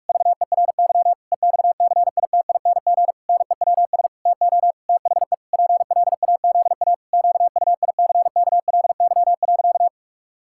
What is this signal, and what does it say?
Morse code, heard as a keyed tone, about 41 wpm, 700 Hz. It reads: VERY EXCITING NEWS TO THE PLAZA QUICKLY.
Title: Pangram Challenge v2 - 45wpm